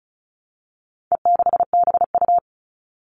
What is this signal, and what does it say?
Morse code, heard as a keyed tone, about 34 wpm, 700 Hz. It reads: E6BU
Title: All Cut Numbers - 4-Digit 35wpm